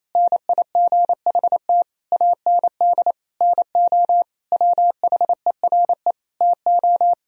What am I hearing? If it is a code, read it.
Morse code, heard as a keyed tone, about 27 wpm, 700 Hz. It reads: NIGHT AND NO WHERE TO
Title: Sentences from Top 1000 Words - Review Part 3 - 28wpm